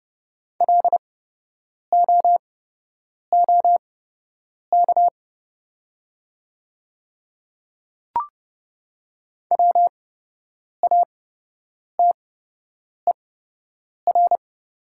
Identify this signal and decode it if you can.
Morse code, heard as a keyed tone, about 29 wpm, spaced as for 8 wpm, 700 Hz. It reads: LOOK WATER
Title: 8x Character Spacing - Top 300 Words - 30wpm